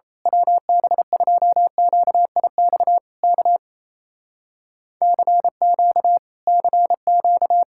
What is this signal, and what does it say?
Morse code, heard as a keyed tone, about 27 wpm, 700 Hz. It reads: WB2QIX K CQ CQ
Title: Speed-Racing Common QSO Phrases - 28wpm